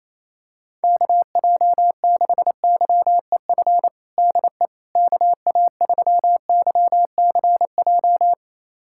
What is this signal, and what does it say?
Morse code, heard as a keyed tone, about 27 wpm, 700 Hz. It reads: KJ6YEF DE KA3YCJ
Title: USA Callsign DE Callsign 28wpm